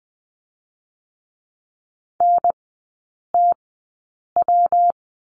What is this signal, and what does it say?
Morse code, heard as a keyed tone, about 20 wpm, spaced as for 8 wpm, 700 Hz. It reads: NTW
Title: 138 Lesson 23e - Letter f - Three Characters 20wpm